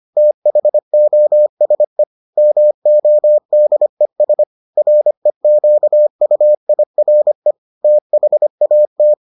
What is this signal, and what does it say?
Morse code, heard as a keyed tone, about 25 wpm, 600 Hz. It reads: THOSE MODES REQUIRE THAT